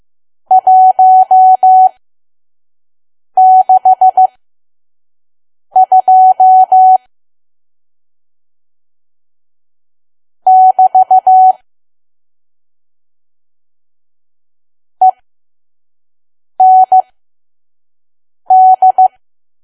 Morse code, heard as a keyed tone, about 15 wpm, spaced as for 5 wpm, 750 Hz. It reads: 162 <BT> END
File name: Archive / 5 WPM files / PM